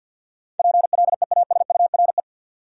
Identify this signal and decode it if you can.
Morse code, heard as a keyed tone, about 48 wpm, 700 Hz. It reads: PLEASURE
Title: Sentences from Top 1600 Words - New Word Drill Alt - 50wpm